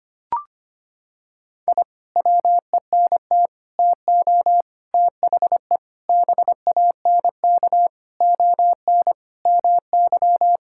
Morse code, heard as a keyed tone, about 25 wpm, 700 Hz. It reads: I WENT TO THE BANK ON MY
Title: Sentences from Top 1000 Words - Review Part 6 - 25wpm